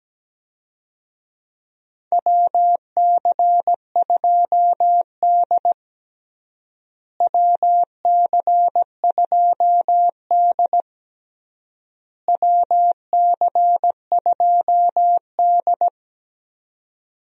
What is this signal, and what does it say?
Morse code, heard as a keyed tone, about 17 wpm, 700 Hz. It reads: WC2D WC2D WC2D
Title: USA Call Signs - 4 Characters - Repeated 3x - 17wpm